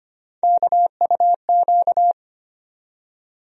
Morse code, heard as a keyed tone, about 25 wpm, 700 Hz. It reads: KUQ
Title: USA Call Sign Postfixes - 3 Letters 25wpm